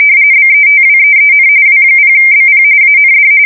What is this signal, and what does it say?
RTTY, about 45.45 baud. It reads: BETWEEN W0BP AND
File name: fsk170 / teletype-RTTY